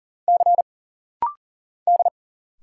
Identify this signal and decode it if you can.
Morse code, heard as a keyed tone, about 41 wpm, 700 Hz. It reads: C D
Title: Single Letter-Number Alt - Rapid-Fire 40wpm